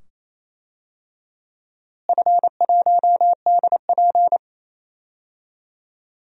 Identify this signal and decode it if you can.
Morse code, heard as a keyed tone, about 28 wpm, 700 Hz. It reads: F1DP
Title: DXCC-100 Call Signs - 4 Characters - Repeated 2x - 28wpm